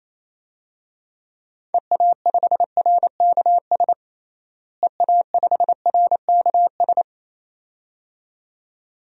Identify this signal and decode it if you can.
Morse code, heard as a keyed tone, about 27 wpm, 700 Hz. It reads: EA5RKS EA5RKS
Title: DXCC-100 Call Signs - 6 Characters - Repeated 2x - 28wpm